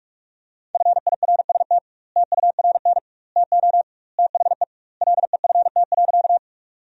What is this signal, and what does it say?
Morse code, heard as a keyed tone, about 42 wpm, 700 Hz. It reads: FIRST TURN TO THE LEFT.
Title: 2x Word-Spacing - Sentences from Top 300 Words - 45wpm